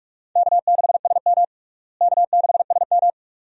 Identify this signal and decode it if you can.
Morse code, heard as a keyed tone, about 41 wpm, 700 Hz. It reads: K6SM K6SM
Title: USA Call Signs - 4 Characters - Repeated 2x - 45wpm